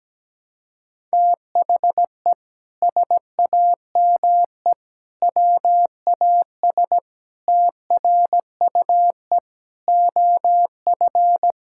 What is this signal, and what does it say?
Morse code, heard as a keyed tone, about 17 wpm, 700 Hz. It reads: THE SAME WAS TRUE OF
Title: Sentences from Top 1000 Words - Review Part 6 - 17wpm